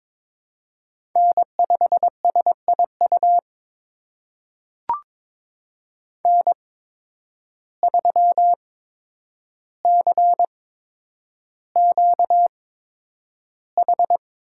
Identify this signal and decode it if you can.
Morse code, heard as a keyed tone, about 22 wpm, spaced as for 9 wpm, 700 Hz. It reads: N5SIU N3CQH
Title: USA Call Signs - 5 Characters - 8x Character Spacing - 22wpm